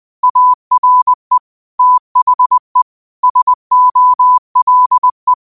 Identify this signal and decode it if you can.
Morse code, heard as a keyed tone, about 20 wpm, 1000 Hz. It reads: ARE THE SOLE